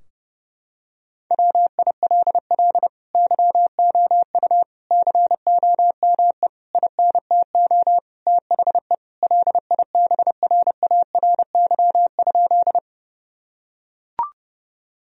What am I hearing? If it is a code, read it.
Morse code, heard as a keyed tone, about 32 wpm, 700 Hz. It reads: WILL YOU COME INTO THE LIBRARY?
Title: Sentences from Top 1000 Words - Review Part 4 - 30wpm